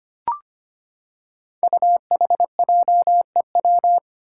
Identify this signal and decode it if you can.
Morse code, heard as a keyed tone, about 25 wpm, 700 Hz. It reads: UHJEW
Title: Five Challenging Letter-Number Combinations 25wpm